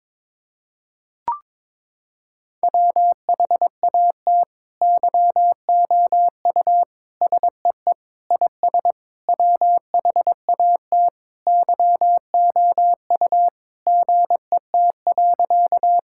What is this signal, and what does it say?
Morse code, heard as a keyed tone, about 22 wpm, 700 Hz. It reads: WHAT YOU SEE IS WHAT YOU GET.
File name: Sentences with 7-Word Phrases - 22wpm